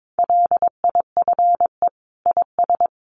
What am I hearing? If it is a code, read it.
Morse code, heard as a keyed tone, about 22 wpm, 700 Hz. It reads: LIFE IS
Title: Sentences from Top 700 Words 22wpm